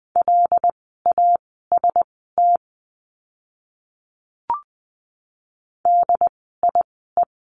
Morse code, heard as a keyed tone, about 20 wpm, spaced as for 14 wpm, 700 Hz. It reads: LAST DIE
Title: 075 Lesson 13f - Letter L - Words 20wpm